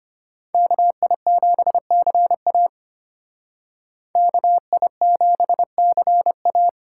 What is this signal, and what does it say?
Morse code, heard as a keyed tone, about 33 wpm, 700 Hz. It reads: KI7CA KI7CA